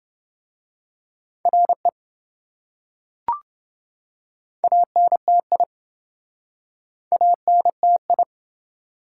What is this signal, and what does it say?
Morse code, heard as a keyed tone, about 27 wpm, 700 Hz. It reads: RE ANTI ANTI